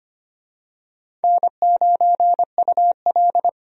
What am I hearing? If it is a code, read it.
Morse code, heard as a keyed tone, about 25 wpm, 700 Hz. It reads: N9UL
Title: USA Call Signs - 4 Characters - Repeated 3x - 25wpm